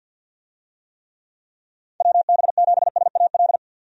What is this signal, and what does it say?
Morse code, heard as a keyed tone, about 46 wpm, 700 Hz. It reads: WB7SAL
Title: USA Call Signs - 6 Characters - 50wpm